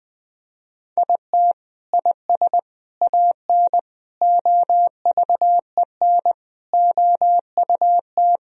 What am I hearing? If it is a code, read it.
Morse code, heard as a keyed tone, about 20 wpm, 700 Hz. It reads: IT IS AN OVEN OUT
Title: Sentences from Top 1000 Words - Review Part 4 - 20wpm